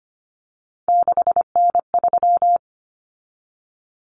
Morse code, heard as a keyed tone, about 25 wpm, 700 Hz. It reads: 6N3
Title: Common Cut Numbers - 3-Digit 25wpm